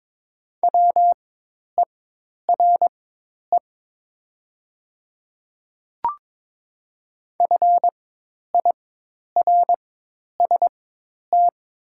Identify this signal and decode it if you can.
Morse code, heard as a keyed tone, about 22 wpm, spaced as for 10 wpm, 700 Hz. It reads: WERE FIRST
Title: 4x Character Spacing - Top 500 Words - 22wpm